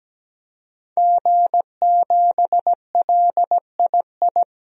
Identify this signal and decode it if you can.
Morse code, heard as a keyed tone, about 17 wpm, 700 Hz. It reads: G7LII
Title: DXCC-100 Call Signs - 5 Characters - 17wpm